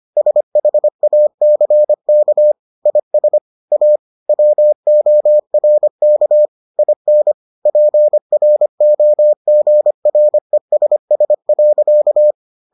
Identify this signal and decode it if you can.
Morse code, heard as a keyed tone, about 25 wpm, 600 Hz. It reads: SHACK IS A WORK IN PROGRESS.